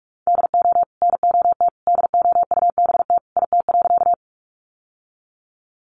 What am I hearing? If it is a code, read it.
Morse code, heard as a keyed tone, about 44 wpm, 700 Hz. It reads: DO NOT DOUBT IT.
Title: Sentences from Top 1500 Words 45wpm